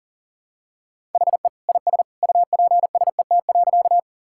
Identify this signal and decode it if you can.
Morse code, heard as a keyed tone, about 40 wpm, 700 Hz. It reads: HE IS UPSET.
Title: Sentences from Top 1000 Words - Review Part 5 - 40wpm